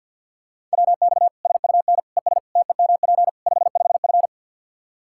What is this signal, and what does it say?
Morse code, heard as a keyed tone, about 50 wpm, 700 Hz. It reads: WX SUN ES TEMP 55F
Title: K1USN SST QSO Practice 50wpm